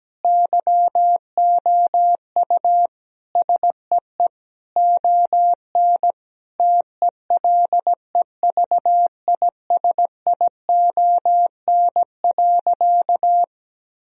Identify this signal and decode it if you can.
Morse code, heard as a keyed tone, about 17 wpm, 700 Hz. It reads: YOU SEE ON TELEVISION.
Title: Sentences from Top 1000 Words - Review Part 4 - 17wpm